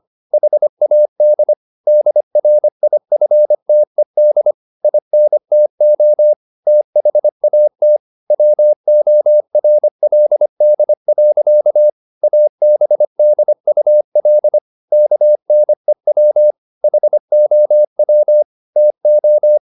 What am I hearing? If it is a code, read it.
Morse code, heard as a keyed tone, about 25 wpm, 600 Hz. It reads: HAD DRIFTED INTO THAT WORLD. ABDUL KNEW HOW TO
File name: Never 0025wpm